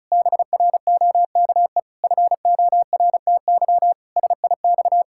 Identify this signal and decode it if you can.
Morse code, heard as a keyed tone, about 34 wpm, 700 Hz. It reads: BROKE FORTY SIX
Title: Pangram Challenge v2 - 35wpm